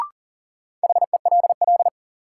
Morse code, heard as a keyed tone, about 40 wpm, 700 Hz. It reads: HELL
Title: Sets of 5 Words Encore - New Word Drill - 40wpm